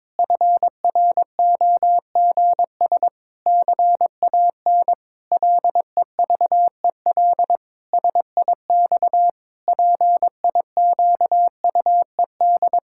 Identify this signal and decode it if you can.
Morse code, heard as a keyed tone, about 22 wpm, 700 Hz. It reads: FROGS CAN LEVEL SIX PIQUED
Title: Pangram Challenge v2 - 22wpm